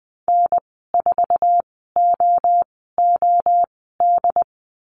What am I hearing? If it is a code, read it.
Morse code, heard as a keyed tone, about 20 wpm, spaced as for 14 wpm, 700 Hz. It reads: N4OOD
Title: 069 Lesson 12g - Letter d - Call Signs 20wpm